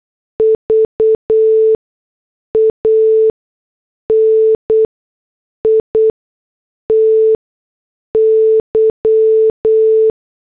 Morse code, heard as a keyed tone, about 8 wpm, spaced as for 6 wpm, 440 Hz. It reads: VANITY